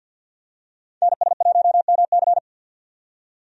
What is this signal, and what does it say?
Morse code, heard as a keyed tone, about 51 wpm, 700 Hz. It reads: NS1MC
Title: USA Call Signs - 5 Characters - 50wpm